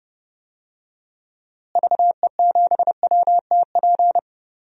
Morse code, heard as a keyed tone, about 31 wpm, 700 Hz. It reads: VE7WTP
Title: DXCC-100 Call Signs - 6 Characters - Repeated 3x - 30wpm